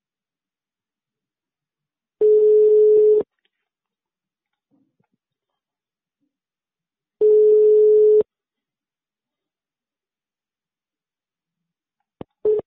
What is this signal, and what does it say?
Ringback tone (Europe, 425 Hz).